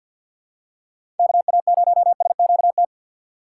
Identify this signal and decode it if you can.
Morse code, heard as a keyed tone, about 51 wpm, 700 Hz. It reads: KA0SQT
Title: USA Call Signs - 2x Character Spacing - 50wpm